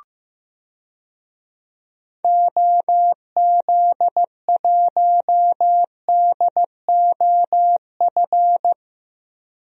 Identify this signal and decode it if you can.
Morse code, heard as a keyed tone, about 15 wpm, 700 Hz. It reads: OZ1DOF
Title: DXCC-100 Call Signs - 6 Characters - Repeated 2x - 15wpm